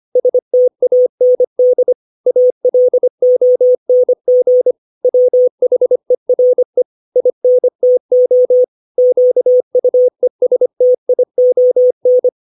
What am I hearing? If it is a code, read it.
Morse code, heard as a keyed tone, about 25 wpm, 500 Hz. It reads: STAND ALONG WHERE INTO QUESTION
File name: 632 words / 632 Words 25WPM